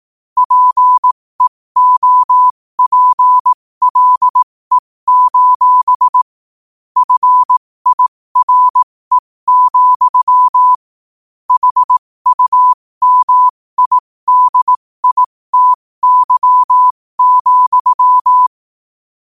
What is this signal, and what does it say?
Morse code, heard as a keyed tone, about 18 wpm, spaced as for 15 wpm, 1000 Hz. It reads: PEOPLE: FIRE, HUMIDITY,
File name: qotdfiles / 15wpm